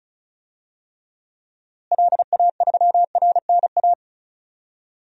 Morse code, heard as a keyed tone, about 34 wpm, 700 Hz. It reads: LA3RNA
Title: DXCC-100 Call Signs - 6 Characters - Repeated 2x - 35wpm